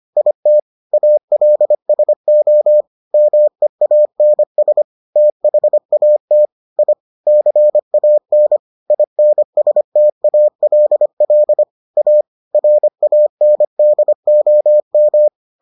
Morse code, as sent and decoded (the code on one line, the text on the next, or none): .. - / .- .-.. ... --- / -- . .- -. ... / - .... .- - / .. / -.-. .- -. / .. -. ... - .- .-.. .-.. / .- / .-. .- -. -.. --- --
IT ALSO MEANS THAT I CAN INSTALL A RANDOM